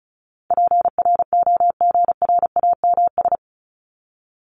Morse code, heard as a keyed tone, about 34 wpm, 700 Hz. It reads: PROGRAMS
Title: ES:S Suffix - Top 100 Words with Base Words 35wpm